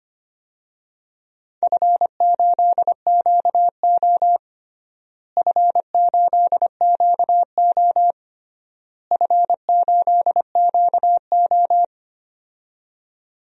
Morse code, heard as a keyed tone, about 25 wpm, 700 Hz. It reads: F8QO F8QO F8QO
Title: DXCC-100 Call Signs - 4 Characters - Repeated 3x - 25wpm